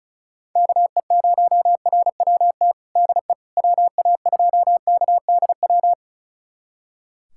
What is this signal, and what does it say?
Morse code, heard as a keyed tone, about 36 wpm, 700 Hz. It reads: KE0RWT DE WA2KDW
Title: USA Callsign DE Callsign - 6 Characters - 35wpm